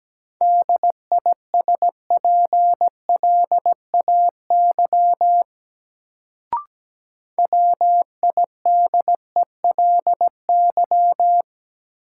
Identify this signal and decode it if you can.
Morse code, heard as a keyed tone, about 17 wpm, 700 Hz. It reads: DISPLAY WIDELY
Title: Sentences from Top 1200 Words - New Word Drill Alt - 17wpm